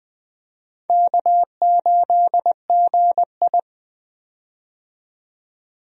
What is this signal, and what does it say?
Morse code, heard as a keyed tone, about 20 wpm, 700 Hz. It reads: K8GI
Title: USA Call Signs - 4 Characters - 20wpm